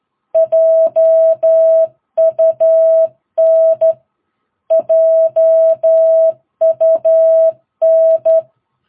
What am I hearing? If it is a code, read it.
Morse code, heard as a keyed tone, about 12 wpm, 650 Hz. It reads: JUN JUN
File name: jh8cbh / 2022cw